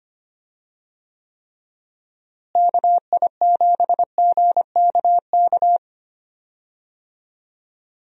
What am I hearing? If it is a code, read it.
Morse code, heard as a keyed tone, about 25 wpm, 700 Hz. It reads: KI7GKK
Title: USA Call Signs - 6 Characters - Repeated 3x - 25wpm